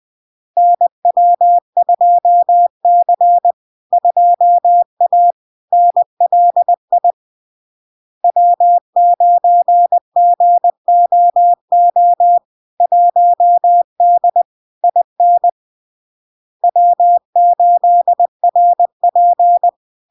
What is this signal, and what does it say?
Morse code, heard as a keyed tone, about 20 wpm, 700 Hz. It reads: NW2C 2A NLI W9GOO 1D IN W8RP
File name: LARK-FD-2022-Log-in-CW-20wpm-700-Hz